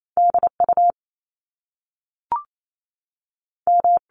Morse code, heard as a keyed tone, about 28 wpm, 700 Hz. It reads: D U M
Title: Top Two Letter Combinations in Words 28wpm